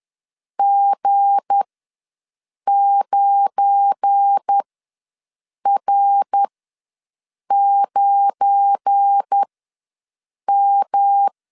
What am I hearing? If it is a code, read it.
Morse code, heard as a keyed tone, about 10 wpm, spaced as for 6 wpm, 800 Hz. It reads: G9R9M
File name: LEZIONI / 7 GF9Z / 7 GF9 ws 10wpm 4034